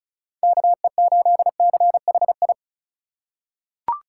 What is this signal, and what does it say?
Morse code, heard as a keyed tone, about 34 wpm, 700 Hz. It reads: KE8CHI
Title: USA Call Signs - 6 Characters - 4x Character Spacing - 35wpm